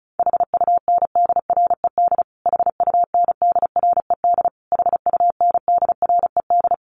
Morse code, heard as a keyed tone, about 34 wpm, 700 Hz. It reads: HUNDRED HUNDRED HUNDRED
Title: Top 500 Words Repeated 35wpm